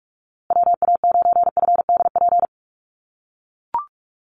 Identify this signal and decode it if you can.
Morse code, heard as a keyed tone, about 44 wpm, 700 Hz. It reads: WA9FDP
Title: USA Call Signs Repeated 2x 45wpm